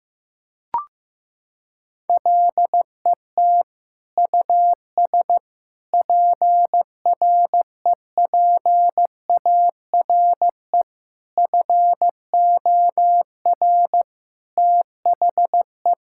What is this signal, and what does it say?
Morse code, heard as a keyed tone, about 15 wpm, 700 Hz. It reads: LET US PREPARE FOR THE
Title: Sentences from Top 1000 Words - Review Part 6 - 15wpm